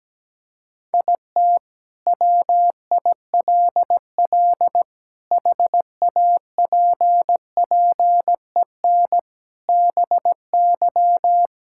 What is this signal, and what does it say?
Morse code, heard as a keyed tone, about 17 wpm, 700 Hz. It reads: IT WILL HAPPEN BY
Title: Sentences from Top 1000 Words - Review Part 5 - 17wpm